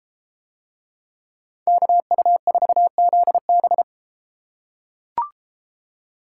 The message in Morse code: KU4ZB

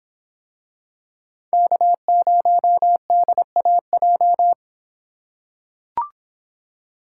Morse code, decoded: K0DAJ

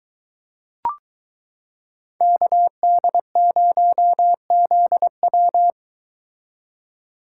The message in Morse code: KD0ZW

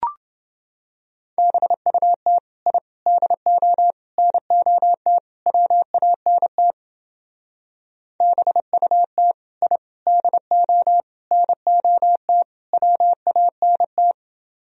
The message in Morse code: BUT I DO NOT WANT BUT I DO NOT WANT

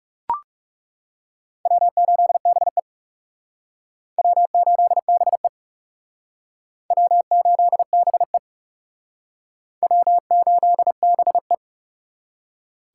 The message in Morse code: W8BE W8BE W8BE W8BE

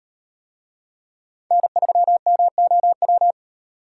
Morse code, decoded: N3MOW

